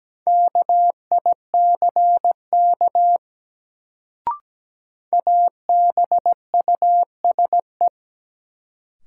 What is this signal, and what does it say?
Morse code, heard as a keyed tone, about 17 wpm, 700 Hz. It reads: KICK ABUSE
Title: Sentences from Top 1300 Words - New Word Drill - 17wpm